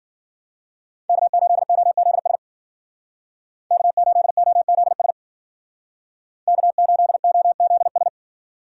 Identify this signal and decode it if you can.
Morse code, heard as a keyed tone, about 49 wpm, 700 Hz. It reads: K8OZS K8OZS K8OZS
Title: Speed Racing - USA 1x3 Call Signs 40wpm